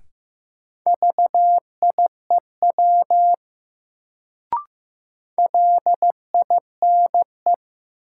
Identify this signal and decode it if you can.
Morse code, heard as a keyed tone, about 15 wpm, 700 Hz. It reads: VIEW LINE
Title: Sets of 5 Words - New Word Drill - 15wpm